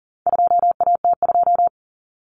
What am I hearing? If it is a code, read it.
Morse code, heard as a keyed tone, about 40 wpm, 700 Hz. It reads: 2AT2
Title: Common Cut Numbers - 4-Digit 40wpm